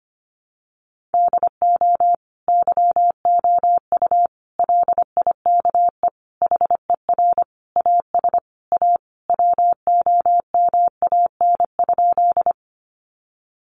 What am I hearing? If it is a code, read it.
Morse code, heard as a keyed tone, about 25 wpm, 700 Hz. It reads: DO YOU LIKE HER AS A WOMAN?
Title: Sentences from Top 1100 Words 25wpm